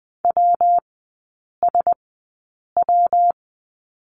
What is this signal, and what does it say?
Morse code, heard as a keyed tone, about 20 wpm, spaced as for 8 wpm, 690 Hz. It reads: WSW
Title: 212 Lesson 34e - Number 8 - Three Characters 20wpm